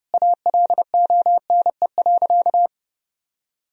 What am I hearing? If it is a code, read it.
Morse code, heard as a keyed tone, about 30 wpm, 700 Hz. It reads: ALONE.
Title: Sentences from Top 1000 Words - Review Part 3 - 30wpm